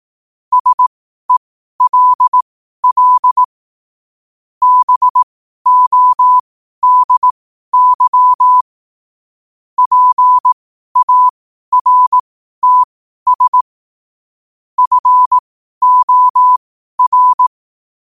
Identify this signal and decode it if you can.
Morse code, heard as a keyed tone, about 18 wpm, spaced as for 12 wpm, 1000 Hz. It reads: SELL BODY PARTS FOR
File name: qotdfiles / 12wpm